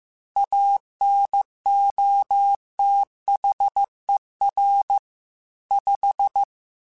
Morse code, heard as a keyed tone, about 15 wpm, 800 Hz. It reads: ANOTHER 5
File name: QSOs / QSOs-mp3s / WA7QOX 1@13